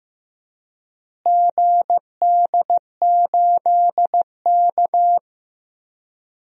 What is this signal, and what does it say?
Morse code, heard as a keyed tone, about 15 wpm, 700 Hz. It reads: GD8K
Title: DXCC-100 Call Signs - 4 Characters - Repeated 3x - 15wpm